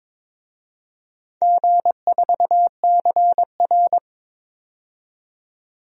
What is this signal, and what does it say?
Morse code, heard as a keyed tone, about 22 wpm, 700 Hz. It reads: G4CR
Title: DXCC-100 Call Signs - 4 Characters - 22wpm